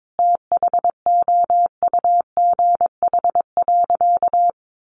Morse code, heard as a keyed tone, about 22 wpm, 700 Hz. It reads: THOUGH.
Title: Sentences from Top 1100 Words 22wpm